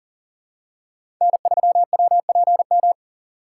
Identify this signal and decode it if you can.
Morse code, heard as a keyed tone, about 40 wpm, 700 Hz. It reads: N3WPM
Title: USA Call Signs - 4x Character Spacing - 40wpm